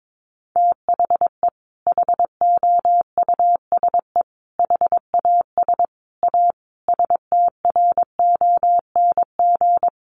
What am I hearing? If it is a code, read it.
Morse code, heard as a keyed tone, about 22 wpm, 700 Hz. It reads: THE HOUSE HAS A STRONG
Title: Sentences from Top 1200 Words 22wpm